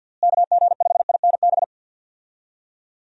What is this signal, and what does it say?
Morse code, heard as a keyed tone, about 51 wpm, 700 Hz. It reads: KG5INB
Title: USA Call Signs - 6 Characters - Repeated 2x - 50wpm